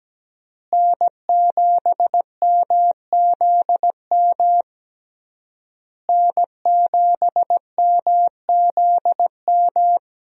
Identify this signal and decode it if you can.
Morse code, heard as a keyed tone, about 17 wpm, 700 Hz. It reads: N7MZM N7MZM
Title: USA Call Signs - 5 Characters - Repeated 3x - 17wpm